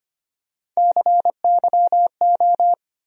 Morse code, heard as a keyed tone, about 25 wpm, 700 Hz. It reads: CYO
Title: USA Call Sign Postfixes - 3 Letters 25wpm